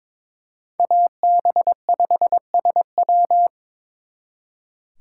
Morse code, heard as a keyed tone, about 25 wpm, 700 Hz. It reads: AB5SW
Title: Speed Racing - USA 2x2 Call Signs 22wpm